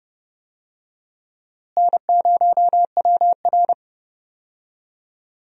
Morse code, heard as a keyed tone, about 30 wpm, 700 Hz. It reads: N0WR
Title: USA Call Signs - 4 Characters - Repeated 2x - 30wpm